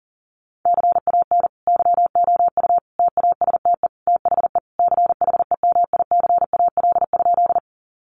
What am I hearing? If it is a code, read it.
Morse code, heard as a keyed tone, about 40 wpm, 700 Hz. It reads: CAN YOU TASTE THE CHEMICAL?
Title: Sentences from Top 1200 Words 40wpm